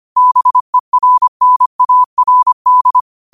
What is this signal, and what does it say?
Morse code, heard as a keyed tone, about 25 wpm, 1000 Hz. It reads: BERNARD
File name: qotdfiles / 25wpm